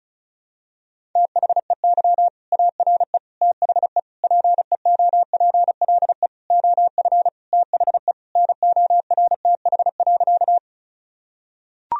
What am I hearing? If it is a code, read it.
Morse code, heard as a keyed tone, about 36 wpm, 700 Hz. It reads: THEY ARE THE PEOPLE OF THE NORTH.
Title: Sentences from Top 1000 Words - Review Part 1 - 35wpm